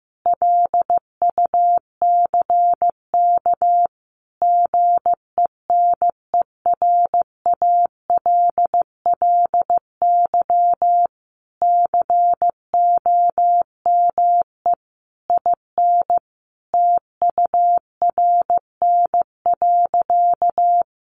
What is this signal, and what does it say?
Morse code, heard as a keyed tone, about 15 wpm, 700 Hz. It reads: LUCK GENERALLY COME IN TURN.
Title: Sentences from Top 1500 Words 15wpm